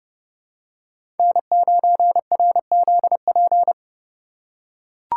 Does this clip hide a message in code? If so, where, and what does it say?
Yes: Morse code from 1.2 to 3.7 s, heard as a keyed tone, about 31 wpm, 700 Hz. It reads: N9RZP